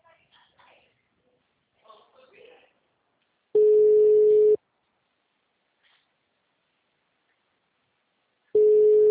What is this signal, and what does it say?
Ringback tone (Europe, 425 Hz), with speech.